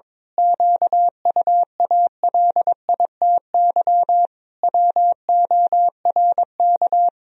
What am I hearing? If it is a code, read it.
Morse code, heard as a keyed tone, about 21 wpm, 700 Hz. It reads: QUALITY WORK
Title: Sentences from Top 1000 Words - Review Part 4 - 22wpm